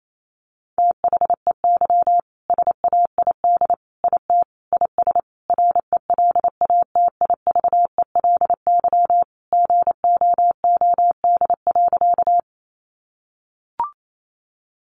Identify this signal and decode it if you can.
Morse code, heard as a keyed tone, about 28 wpm, 700 Hz. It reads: THEY SAID IT IS RELATIVELY GOOD.
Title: Sentences from Top 1100 Words 28wpm